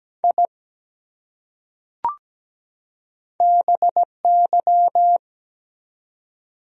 Morse code, heard as a keyed tone, about 17 wpm, 700 Hz. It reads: I BY
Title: Top 100 Words 17wpm